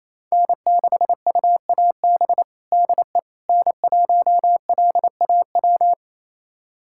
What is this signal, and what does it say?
Morse code, heard as a keyed tone, about 28 wpm, 700 Hz. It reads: N6UAB DE N1LAW
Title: USA Callsign DE Callsign - 5 Characters - 28wpm